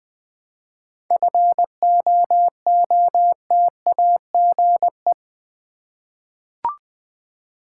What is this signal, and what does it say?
Morse code, heard as a keyed tone, about 20 wpm, 700 Hz. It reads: FOOTAGE
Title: 7 Letter Words - First 3 Letters Given - 20wpm